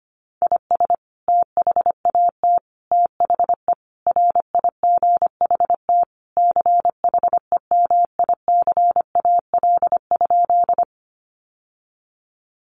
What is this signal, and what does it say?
Morse code, heard as a keyed tone, about 25 wpm, 700 Hz. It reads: IS THAT THE RIGHT CHEMICAL?
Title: Sentences from Top 1200 Words 25wpm